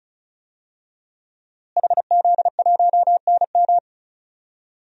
Morse code, heard as a keyed tone, about 34 wpm, 700 Hz. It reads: HZ1NM
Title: DXCC-100 Call Signs - 5 Characters - Repeated 2x - 35wpm